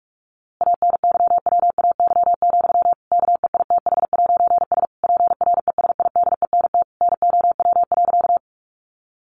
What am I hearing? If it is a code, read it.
Morse code, heard as a keyed tone, about 44 wpm, 700 Hz. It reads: ANYWAY, KEITH'S PRESIDENT NOW.
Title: A Sound Of Thunder 45wpm